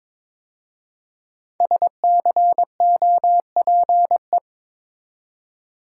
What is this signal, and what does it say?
Morse code, heard as a keyed tone, about 22 wpm, 700 Hz. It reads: SCOPE
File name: 5 Letter Words - First 3 Letters Given - 22wpm